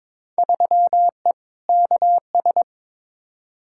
Morse code, heard as a keyed tone, about 22 wpm, 700 Hz. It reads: 3E KS
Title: Field Day Exchanges Encore - 22wpm